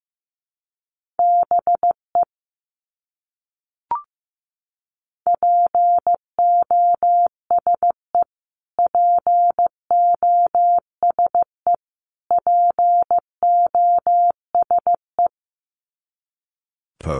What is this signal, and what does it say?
Morse code, heard as a keyed tone, about 15 wpm, 700 Hz. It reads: BE POSE POSE POSE
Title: Top 500 Words Repeated 15wpm